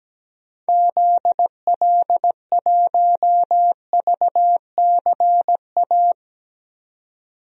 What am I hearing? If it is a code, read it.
Morse code, heard as a keyed tone, about 17 wpm, 700 Hz. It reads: ZL1VCA
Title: DXCC-100 Call Signs - 6 Characters - 17wpm